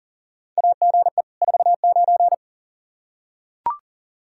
Morse code, heard as a keyed tone, about 40 wpm, 700 Hz. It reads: AGE 49
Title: K1USN SST QSO Practice 40wpm